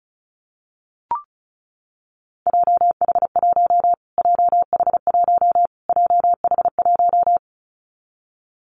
Morse code, heard as a keyed tone, about 34 wpm, 700 Hz. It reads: JH1 JH1 JH1
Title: Callsign Prefixes for Top 25 DXCC - Repeated 3x - 35wpm